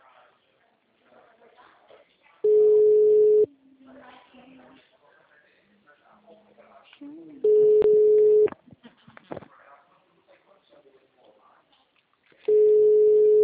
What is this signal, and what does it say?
Ringback tone (Europe, 425 Hz), with speech.